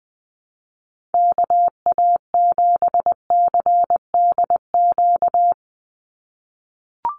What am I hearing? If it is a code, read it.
Morse code, heard as a keyed tone, about 20 wpm, 700 Hz. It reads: KA7CDQ
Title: USA Call Signs 20wpm